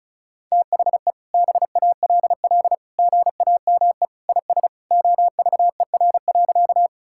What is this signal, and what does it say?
Morse code, heard as a keyed tone, about 36 wpm, 700 Hz. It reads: THE BALL GAME IS OVER.
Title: Sentences from Top 1000 Words - Review Part 6 - 35wpm